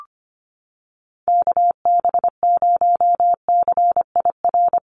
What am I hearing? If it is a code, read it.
Morse code, heard as a keyed tone, about 25 wpm, 700 Hz. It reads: KB0CIR
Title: USA Call Signs Repeated 3x 25wpm